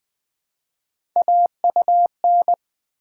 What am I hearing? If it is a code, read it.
Morse code, heard as a keyed tone, about 20 wpm, 700 Hz. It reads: AUN
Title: Build A Word - 1 Letter At A Time - 4-Letter Words - 20wpm